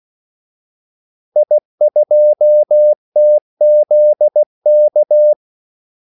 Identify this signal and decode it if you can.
Morse code, heard as a keyed tone, about 16 wpm, 600 Hz. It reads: I2TZK
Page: Kaldesignaler 16 wpm